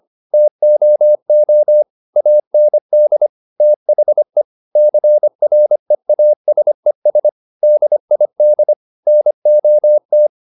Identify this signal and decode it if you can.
Morse code, heard as a keyed tone, about 25 wpm, 600 Hz. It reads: TOO AND THE CREASES DID NOT
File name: the old man and the sea part 1 25wpm